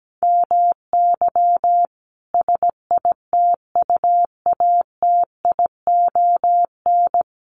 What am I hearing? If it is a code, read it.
Morse code, heard as a keyed tone, about 17 wpm, 700 Hz. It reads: MY SITUATION
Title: Sentences from Top 1200 Words 17wpm